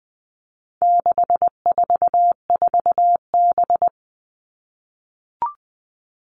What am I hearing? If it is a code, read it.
Morse code, heard as a keyed tone, about 20 wpm, 700 Hz. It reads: 644B
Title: All Cut Numbers - 4-Digit 20wpm